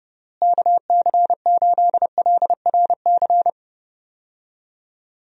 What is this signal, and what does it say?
Morse code, heard as a keyed tone, about 30 wpm, 700 Hz. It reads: KC8LRC
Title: USA Call Signs - 6 Characters - Repeated 3x - 30wpm